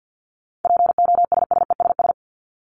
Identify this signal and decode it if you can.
Morse code, heard as a keyed tone, about 50 wpm, 700 Hz. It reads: POSSESS
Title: Top 1751-2000 Words 50wpm